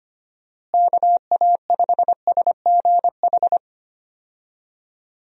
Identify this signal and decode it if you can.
Morse code, heard as a keyed tone, about 25 wpm, 700 Hz. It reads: KA5SGH
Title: USA Call Signs - 6 Characters - Repeated 2x - 25wpm